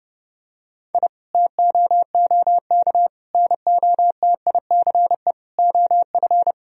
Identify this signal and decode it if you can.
Morse code, heard as a keyed tone, about 32 wpm, 700 Hz. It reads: I TOOK NOTICE OF
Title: Sentences from Top 1000 Words - Review Part 2 - 30wpm